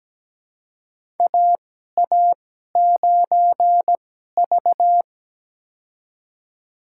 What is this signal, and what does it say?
Morse code, heard as a keyed tone, about 17 wpm, spaced as for 12 wpm, 700 Hz. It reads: AA9V
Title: USA Call Signs - 4 Characters - 2x Character Spacing - 17wpm